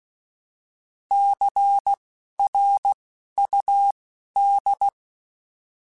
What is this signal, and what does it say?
Morse code, heard as a keyed tone, about 16 wpm, spaced as for 12 wpm, 800 Hz. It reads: CRUD
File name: Alphabetical / Alphabetical mp3s / Cs @10